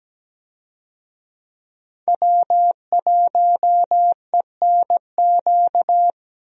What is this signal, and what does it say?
Morse code, heard as a keyed tone, about 17 wpm, 700 Hz. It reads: W1ENQ
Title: USA Call Signs - 5 Characters - Repeated 3x - 17wpm